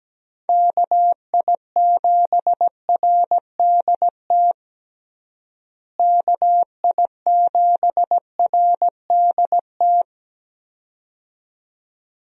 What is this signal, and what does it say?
Morse code, heard as a keyed tone, about 17 wpm, 700 Hz. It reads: KI7RDT KI7RDT
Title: USA Call Signs - 6 Characters - Repeated 2x - 17wpm